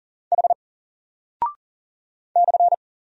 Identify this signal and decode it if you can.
Morse code, heard as a keyed tone, about 41 wpm, 700 Hz. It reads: H /
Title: Single Character (Letter:Number:Punctuation:Prosign) 40wpm